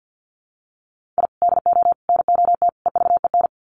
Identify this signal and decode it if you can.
Morse code, heard as a keyed tone, about 50 wpm, 700 Hz. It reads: I DO NOT EVEN
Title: Sets of 4 Words 50wpm